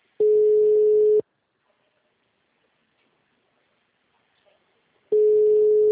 Ringback tone (Europe, 425 Hz).